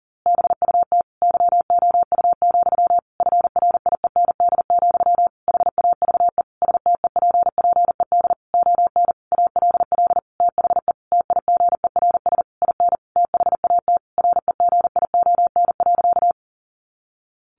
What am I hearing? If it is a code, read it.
Morse code, heard as a keyed tone, about 40 wpm, 700 Hz. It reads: BUT YOU, FRIEND, HAVE STEPPED ON ALL THE TIGERS IN THAT REGION.
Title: A Sound Of Thunder 40wpm